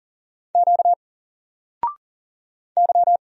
Morse code, heard as a keyed tone, about 40 wpm, 700 Hz. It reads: Q Y
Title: Single Letter-Number 40wpm